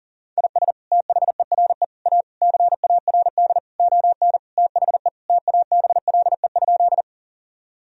Morse code, heard as a keyed tone, about 40 wpm, 700 Hz. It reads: IS THERE A CARD ON THE TABLE?
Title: Sentences from Top 1000 Words - Review Part 4 - 40wpm